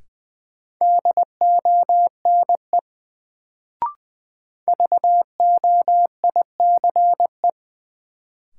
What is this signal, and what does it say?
Morse code, heard as a keyed tone, about 20 wpm, 700 Hz. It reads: DONE VOICE
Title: Sentences from Top 500 Words - New Word Drill - 20wpm